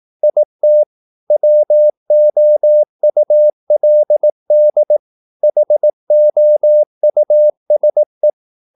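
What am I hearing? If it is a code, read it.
Morse code, heard as a keyed tone, about 18 wpm, 600 Hz. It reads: IT WOULD HOUSE
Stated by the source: Morse practice at 18 wpm, English text, from CW med Gnister.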